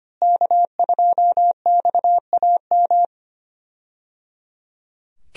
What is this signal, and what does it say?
Morse code, heard as a keyed tone, about 25 wpm, 700 Hz. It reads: K2XAM
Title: USA Call Signs - 5 Characters - Repeated 3x - 25wpm